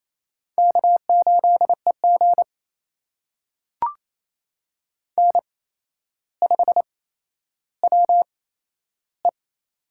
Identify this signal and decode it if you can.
Morse code, heard as a keyed tone, about 28 wpm, spaced as for 12 wpm, 700 Hz. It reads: K8EG N5WE
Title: USA Call Signs - 4 Characters - 8x Character Spacing - 28wpm